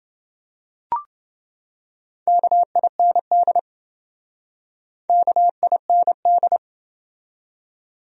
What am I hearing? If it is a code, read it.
Morse code, heard as a keyed tone, about 28 wpm, 700 Hz. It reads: KIND KIND